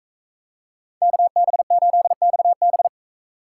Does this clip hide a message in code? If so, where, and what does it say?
Yes: Morse code from 1.0 to 2.9 s, heard as a keyed tone, about 40 wpm, 700 Hz. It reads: KB8XB